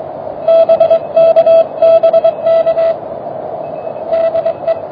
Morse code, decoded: BK BK DE